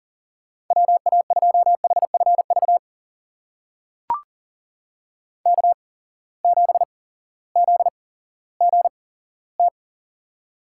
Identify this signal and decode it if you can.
Morse code, heard as a keyed tone, about 40 wpm, spaced as for 18 wpm, 700 Hz. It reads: WA2HFV K7ZGT
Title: USA Call Signs - 8x Character Spacing - 40wpm